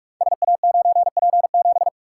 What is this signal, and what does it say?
Morse code, heard as a keyed tone, about 44 wpm, 690 Hz. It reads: SA9P7
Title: Code Groups - 5 Letter:Numbers - 45wpm